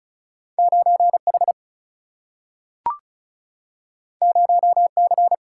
Morse code, heard as a keyed tone, about 36 wpm, 700 Hz. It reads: 9 H 0 C
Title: Single-Letter Single-Number 35wpm